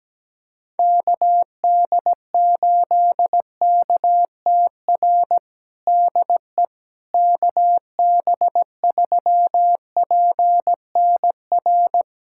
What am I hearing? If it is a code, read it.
Morse code, heard as a keyed tone, about 17 wpm, 700 Hz. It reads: KD8KTR DE KB3PNR
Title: USA Callsign DE Callsign - 6 Characters - 17wpm